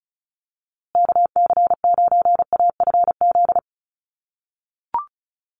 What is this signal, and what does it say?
Morse code, heard as a keyed tone, about 34 wpm, 700 Hz. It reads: KC9AFZ
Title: USA Call Signs Repeated 3x 35wpm